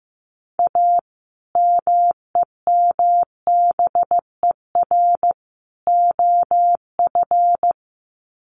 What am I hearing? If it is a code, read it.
Morse code, heard as a keyed tone, about 15 wpm, 700 Hz. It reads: A MEMBER OF
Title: Sets of 3 Words 15wpm